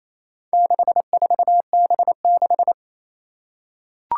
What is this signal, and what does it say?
Morse code, heard as a keyed tone, about 27 wpm, 700 Hz. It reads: 64B6
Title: Four Challenging Letter-Number Combinations 28wpm